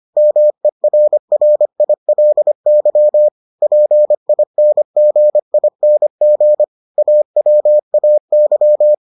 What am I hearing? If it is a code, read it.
Morse code, heard as a keyed tone, about 25 wpm, 600 Hz. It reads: MERRILY PINGING AWAY